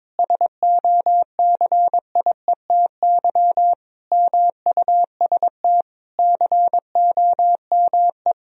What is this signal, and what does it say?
Morse code, heard as a keyed tone, about 22 wpm, 700 Hz. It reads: SOCIETY MUST COME
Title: Sentences from Top 1000 Words - Review Part 4 - 22wpm